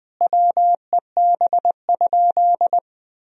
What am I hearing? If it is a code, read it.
Morse code, heard as a keyed tone, about 20 wpm, 700 Hz. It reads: WEB?
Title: Sentences from Top 1000 Words - Review Part 4 - 20wpm